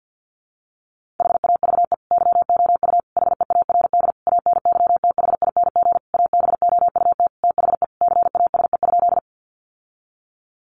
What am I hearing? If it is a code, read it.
Morse code, heard as a keyed tone, about 50 wpm, 700 Hz. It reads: HAVE YOU HEARD ANYTHING ABOUT THE CASE?
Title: Sentences from Top 1000 Words 50wpm